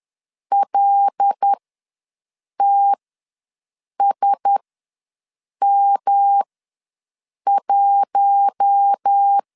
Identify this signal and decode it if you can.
Morse code, heard as a keyed tone, about 10 wpm, spaced as for 6 wpm, 800 Hz. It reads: LTSM1
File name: LEZIONI / 5 LBVC / 5 LB ws 10wpm 3616